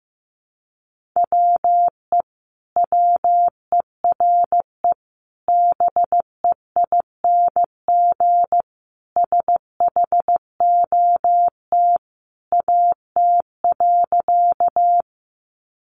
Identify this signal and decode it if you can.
Morse code, heard as a keyed tone, about 15 wpm, 700 Hz. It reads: WE WERE BEING SHOT AT.
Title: Sentences from Top 1100 Words 15wpm